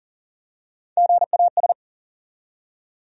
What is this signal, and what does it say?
Morse code, heard as a keyed tone, about 40 wpm, 700 Hz. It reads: GAS
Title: Three Letter Words - 40wpm